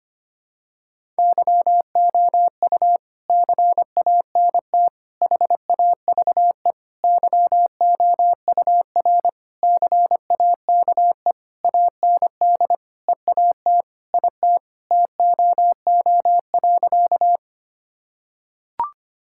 Morse code, heard as a keyed tone, about 25 wpm, 700 Hz. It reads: YOU CANT HAVE YOUR CAKE AND EAT IT TOO.
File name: practice / 25wpm / Sentences with 9-Word Phrases - 25wpm